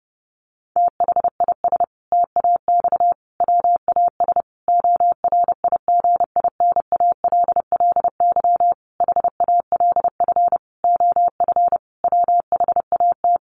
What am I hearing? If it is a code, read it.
Morse code, heard as a keyed tone, about 30 wpm, 700 Hz. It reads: THIS TAX WAS ORIGINALLY HALF OF WHAT
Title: Sentences from Top 1500 Words 30wpm